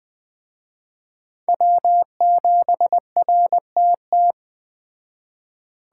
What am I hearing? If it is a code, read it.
Morse code, heard as a keyed tone, about 20 wpm, 700 Hz. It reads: W7RTT